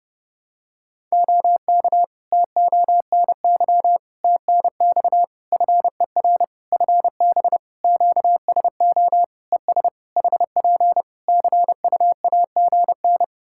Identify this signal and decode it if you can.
Morse code, heard as a keyed tone, about 30 wpm, 700 Hz. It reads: OK TONY TNX FER FB QSO ES HP CUAGN
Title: Practice QSO - Exchange 4 - Ending - 30wpm